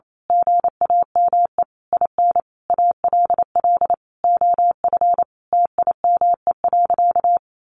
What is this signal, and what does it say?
Morse code, heard as a keyed tone, about 28 wpm, 700 Hz. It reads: GAME IN ALL OF TIME.
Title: A Sound Of Thunder 28wpm